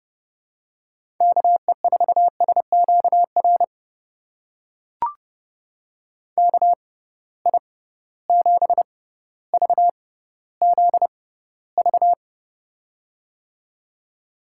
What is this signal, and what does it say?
Morse code, heard as a keyed tone, about 30 wpm, spaced as for 15 wpm, 700 Hz. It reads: KE4SQR KI7VZV
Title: USA Call Signs - 6 Characters - 6x Character Spacing - 30wpm